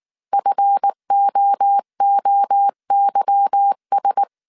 Morse code, heard as a keyed tone, about 19 wpm, 800 Hz. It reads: FOOYS
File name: CW / DET / 1 9 1500C S A 18wpm 2459